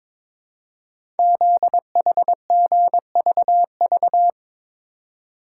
Morse code, heard as a keyed tone, about 23 wpm, 700 Hz. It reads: ZHGVV
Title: Code Groups - 5 Letters - 22wpm